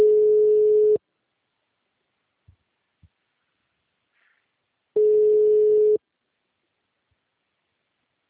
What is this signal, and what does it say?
Ringback tone (Europe, 425 Hz).
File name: out- -2020- - - .1509